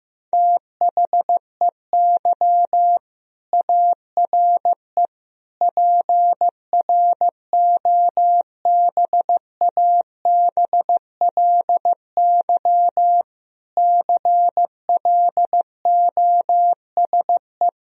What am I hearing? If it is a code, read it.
Morse code, heard as a keyed tone, about 15 wpm, 700 Hz. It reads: THEY ARE PROBABLY CLOSE
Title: Sentences from Top 1000 Words - Review Part 4 - 15wpm